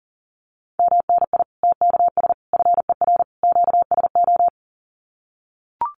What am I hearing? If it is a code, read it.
Morse code, heard as a keyed tone, about 40 wpm, 700 Hz. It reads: MNI TKS FER QSO
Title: Common QSO Phrases 40wpm